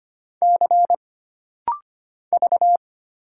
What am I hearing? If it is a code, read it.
Morse code, heard as a keyed tone, about 25 wpm, 700 Hz. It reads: C V
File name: Single Character (Letter:Number:Punctuation:Prosign) - Rapid-Fire - 25wpm